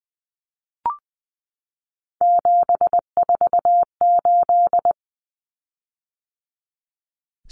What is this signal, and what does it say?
Morse code, heard as a keyed tone, about 20 wpm, 700 Hz. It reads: 748
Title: Common Cut Numbers - 3-Digit 20wpm